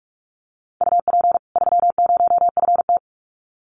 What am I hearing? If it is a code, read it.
Morse code, heard as a keyed tone, about 44 wpm, 700 Hz. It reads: UP 30FT
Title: Common QSO Phrases 45wpm